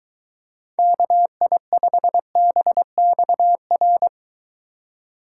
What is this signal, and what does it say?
Morse code, heard as a keyed tone, about 23 wpm, 700 Hz. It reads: KI5BXR